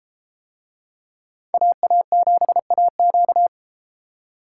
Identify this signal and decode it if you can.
Morse code, heard as a keyed tone, about 31 wpm, 700 Hz. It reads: AA7AQ